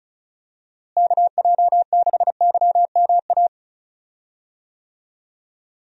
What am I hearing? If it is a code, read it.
Morse code, heard as a keyed tone, about 36 wpm, 700 Hz. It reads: KJ6YMA
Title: USA Call Signs - 6 Characters - Repeated 3x - 35wpm